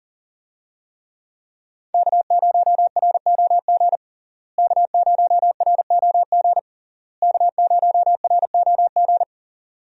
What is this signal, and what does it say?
Morse code, heard as a keyed tone, about 40 wpm, 700 Hz. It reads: K0ROG K0ROG K0ROG
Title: USA Call Signs - 5 Characters - Repeated 3x - 40wpm